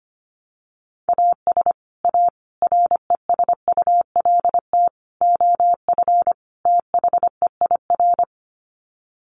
Morse code, heard as a keyed tone, about 25 wpm, 700 Hz. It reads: AS A RESULT OF THEIR
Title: Sets of 5 Words Encore2 25wpm